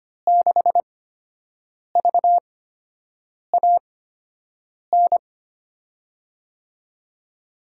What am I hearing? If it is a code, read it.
Morse code, heard as a keyed tone, about 25 wpm, 700 Hz. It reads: 6 V A N
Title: USA Call Signs - 8x Character Spacing - 25wpm